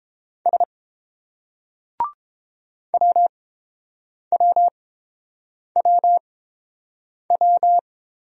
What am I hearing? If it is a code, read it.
Morse code, heard as a keyed tone, about 29 wpm, spaced as for 7 wpm, 700 Hz. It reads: S WWWW